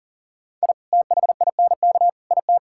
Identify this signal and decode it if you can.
Morse code, heard as a keyed tone, about 40 wpm, 700 Hz. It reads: I THINK IT
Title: Sentences from Top 1000 Words - Review Part 5 - 40wpm